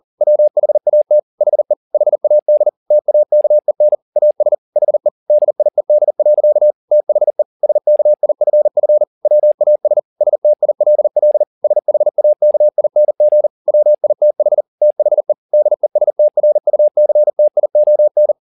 Morse code, heard as a keyed tone, about 40 wpm, 600 Hz. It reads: WHAT HE HAD TAKEN AS HE DIED. THE SKIFF WAS STILL SHAKING WITH THE DESTRUCTION
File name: the old man and the sea 2 40wpm